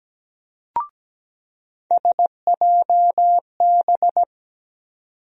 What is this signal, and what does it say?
Morse code, heard as a keyed tone, about 17 wpm, 700 Hz. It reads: SJB
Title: USA Call Sign Postfixes - 3 Letters 17wpm